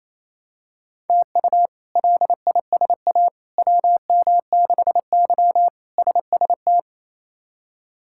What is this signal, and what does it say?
Morse code, heard as a keyed tone, about 28 wpm, 700 Hz. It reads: TU LISA WM6Y SST
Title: K1USN SST QSO Practice - 28wpm